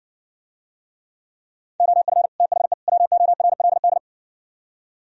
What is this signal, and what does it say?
Morse code, heard as a keyed tone, about 60 wpm, 700 Hz. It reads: OF THE WORLD